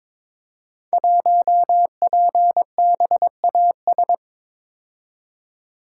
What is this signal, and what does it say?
Morse code, heard as a keyed tone, about 22 wpm, 700 Hz. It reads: 1PBAS
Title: Code Groups - 5 Letter:Numbers - 22wpm